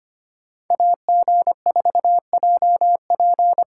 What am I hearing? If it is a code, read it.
Morse code, heard as a keyed tone, about 25 wpm, 700 Hz. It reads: AG4JP
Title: USA Call Signs - 5 Characters - 3x Character Spacing - 25wpm